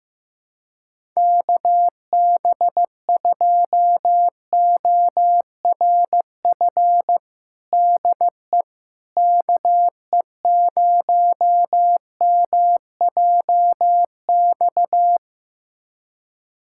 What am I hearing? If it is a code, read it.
Morse code, heard as a keyed tone, about 15 wpm, 700 Hz. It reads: KB2ORF DE KE0MJX
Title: USA Callsign DE Callsign - 6 Characters - 15wpm